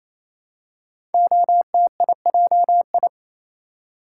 Morse code, heard as a keyed tone, about 27 wpm, 700 Hz. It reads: OTIJI
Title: Code Groups - 5 Letters - 28wpm